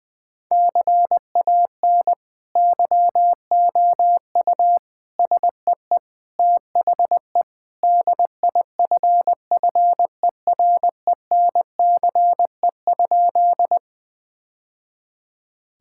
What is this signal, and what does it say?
Morse code, heard as a keyed tone, about 20 wpm, 700 Hz. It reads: CAN YOU SEE THE DIFFERENCE?
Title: Sentences with 2-Word Phrases - Spoken Beginning - 20wpm